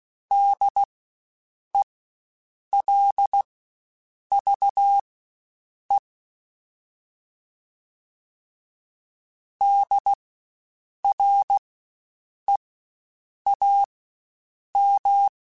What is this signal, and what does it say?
Morse code, heard as a keyed tone, about 16 wpm, spaced as for 7 wpm, 800 Hz. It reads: DELVE DREAM
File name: Spaced Alphas / Spaced Alphas mp3s / Ds sp @15